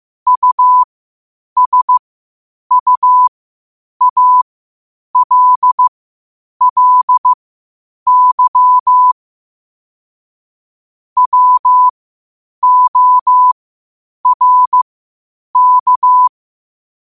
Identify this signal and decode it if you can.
Morse code, heard as a keyed tone, about 15 wpm, spaced as for 8 wpm, 1000 Hz. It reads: USUALLY WORK
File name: qotdfiles / 8wpm